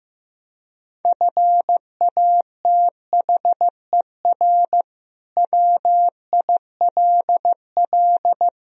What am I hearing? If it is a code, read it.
Morse code, heard as a keyed tone, about 15 wpm, 700 Hz. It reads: FATHER WILL
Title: Sentences from Top 1000 Words - Review Part 3 - 15wpm